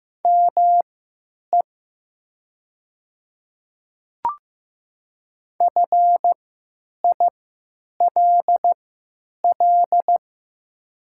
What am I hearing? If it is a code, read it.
Morse code, heard as a keyed tone, about 15 wpm, spaced as for 9 wpm, 700 Hz. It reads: ME FILL
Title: 3x Character Spacing - Top 500 Words - 15wpm